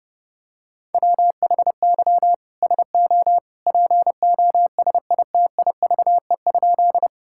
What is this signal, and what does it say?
Morse code, heard as a keyed tone, about 32 wpm, 700 Hz. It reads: WHY SO POSITIVE?
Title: Sentences from Top 1000 Words - Review Part 5 - 30wpm